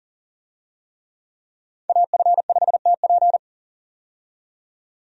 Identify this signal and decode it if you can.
Morse code, heard as a keyed tone, about 40 wpm, 700 Hz. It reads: AF5TP
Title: USA Call Signs - 5 Characters - 40wpm